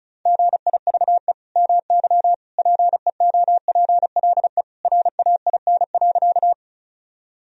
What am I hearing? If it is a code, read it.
Morse code, heard as a keyed tone, about 34 wpm, 700 Hz. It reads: GIVE MY PEOPLE RAIN.
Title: Sentences from Top 1000 Words - Review Part 1 - 35wpm